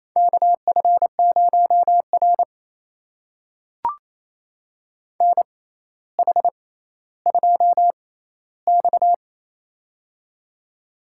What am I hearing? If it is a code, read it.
Morse code, heard as a keyed tone, about 29 wpm, spaced as for 15 wpm, 700 Hz. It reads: KF0R NH2X
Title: USA Call Signs - 4 Characters - 6x Character Spacing - 28wpm